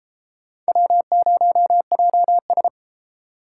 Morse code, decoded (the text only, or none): W0JS